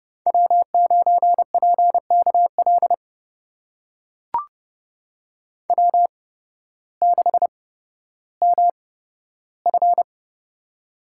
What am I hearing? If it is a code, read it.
Morse code, heard as a keyed tone, about 31 wpm, spaced as for 14 wpm, 700 Hz. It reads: W9PKL W6MF
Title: USA Call Signs - 8x Character Spacing - 30wpm